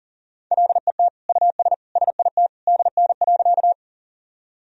Morse code, heard as a keyed tone, about 40 wpm, 700 Hz. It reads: LET US SIT DN.
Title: Sentences from Top 500 Words and Abbreviations 40wpm